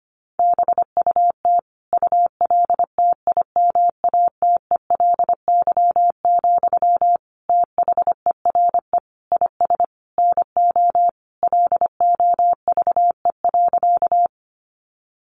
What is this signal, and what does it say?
Morse code, heard as a keyed tone, about 25 wpm, 700 Hz. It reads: BUT ULTIMATELY, THERE IS NO LOVE.
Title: Sentences from Top 1200 Words 25wpm